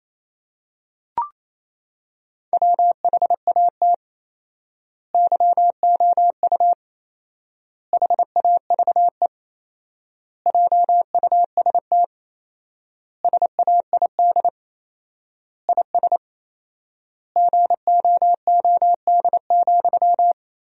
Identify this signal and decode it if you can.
Morse code, heard as a keyed tone, about 28 wpm, 700 Hz. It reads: WHAT YOU HAVE JUST SAID IS GOOD,
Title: 4x Word-Spacing - Sentences from Top 200 Words - 28wpm